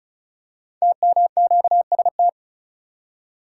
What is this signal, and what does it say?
Morse code, heard as a keyed tone, about 36 wpm, 700 Hz. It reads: TMQST
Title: Code Groups - 5 Letter:Numbers - 35wpm